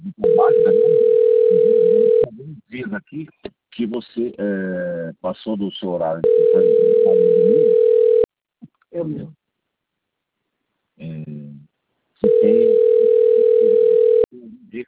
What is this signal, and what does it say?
Ringback tone (North America), 440 + 480 Hz, with speech.
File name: rg-1600-unknown- - - .28445